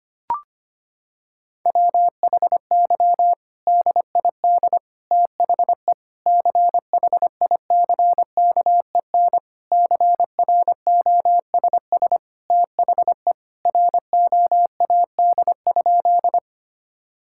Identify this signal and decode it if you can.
Morse code, heard as a keyed tone, about 25 wpm, 700 Hz. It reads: WHY DID THE CHICKEN CROSS THE ROAD?
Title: Sentences from Top 1000 Words - Review Part 5 - 25wpm